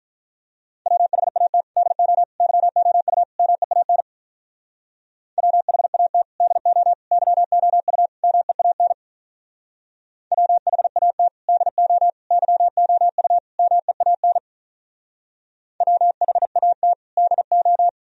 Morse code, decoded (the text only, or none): WHAT DO YOU MEAN WHAT DO YOU MEAN WHAT DO YOU MEAN WHAT DO